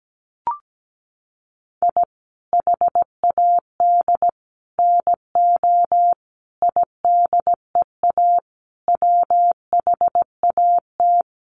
Morse code, heard as a keyed tone, about 17 wpm, 700 Hz. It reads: I HAD NO IDEA WHAT
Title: Sets of 5 Words Encore 17wpm